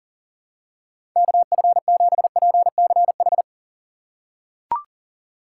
Morse code, heard as a keyed tone, about 40 wpm, 700 Hz. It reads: KF7PCH
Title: USA Call Signs - 6 Characters - 4x Character Spacing - 40wpm